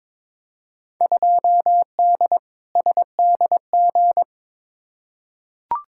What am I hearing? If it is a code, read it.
Morse code, heard as a keyed tone, about 22 wpm, 700 Hz. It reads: 2D SDG
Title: Field Day Exchanges Encore - 22wpm